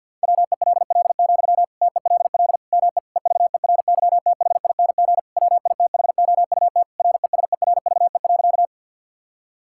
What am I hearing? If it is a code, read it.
Morse code, heard as a keyed tone, about 50 wpm, 700 Hz. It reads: WELL, TELL ME EVERYTHING WITHOUT RESERVE.
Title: Sentences from Top 1000 Words - Review Part 6 - 50wpm